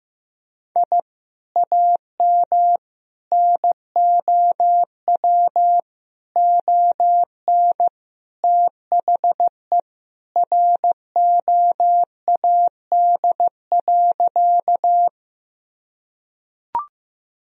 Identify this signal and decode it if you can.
Morse code, heard as a keyed tone, about 15 wpm, 700 Hz. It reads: I AM NOW ON THE ROAD.
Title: Sentences from Top 1000 Words - Review Part 2 - 15wpm